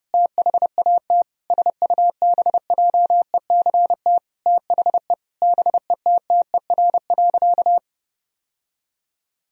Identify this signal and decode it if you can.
Morse code, heard as a keyed tone, about 30 wpm, 700 Hz. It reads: THAT SUBJECT THE BETTER.
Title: Sentences from Top 1000 Words - Review Part 2 - 30wpm